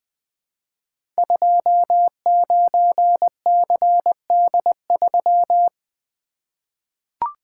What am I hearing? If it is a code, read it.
Morse code, heard as a keyed tone, about 20 wpm, 700 Hz. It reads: 29CD3
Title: Code Groups - 5 Letter:Numbers - 20wpm